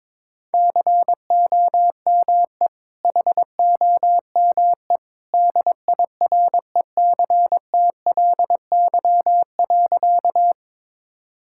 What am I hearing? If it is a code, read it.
Morse code, heard as a keyed tone, about 22 wpm, 700 Hz. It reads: COME HOME DIRECTLY.
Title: Sentences from Top 1000 Words - Review Part 5 - 22wpm